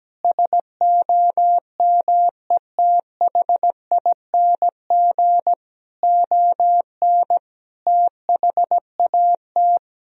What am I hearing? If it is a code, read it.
Morse code, heard as a keyed tone, about 17 wpm, 700 Hz. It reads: SOMETHING ON THAT
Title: Sentences from Top 1000 Words - Review Part 3 - 17wpm